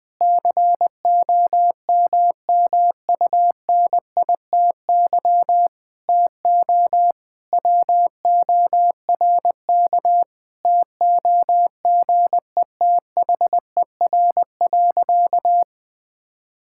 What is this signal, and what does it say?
Morse code, heard as a keyed tone, about 20 wpm, 700 Hz. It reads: COMMUNITY TO WORK TOGETHER.
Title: Sentences with 7-Word Phrases - Spoken Beginning - 20wpm